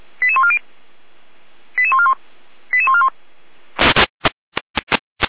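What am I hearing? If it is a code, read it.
Five-tone selective call (ZVEI), heard as quick runs of five tones, tones of about 70 ms. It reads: 88140 88141 88141
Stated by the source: ZVEI five-tone selective call received on an SDR.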